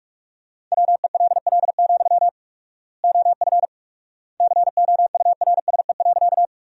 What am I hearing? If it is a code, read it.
Morse code, heard as a keyed tone, about 45 wpm, 700 Hz. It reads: WELL, OF COURSE.
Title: 4x Word-Spacing - Sentences from Top 500 Words - 45wpm